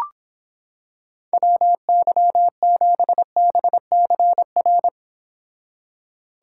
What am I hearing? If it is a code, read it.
Morse code, heard as a keyed tone, about 26 wpm, 700 Hz. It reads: WY7BCR